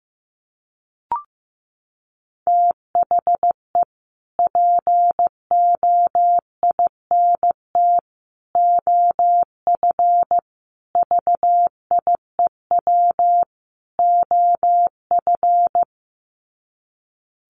Morse code, heard as a keyed tone, about 15 wpm, 700 Hz. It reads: THE POINT OF VIEW OF
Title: Sets of 5 Words 15wpm